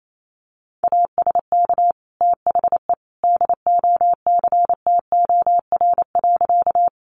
Morse code, heard as a keyed tone, about 28 wpm, 700 Hz. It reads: ASK THE DOCTOR.
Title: Sentences from Top 1500 Words 28wpm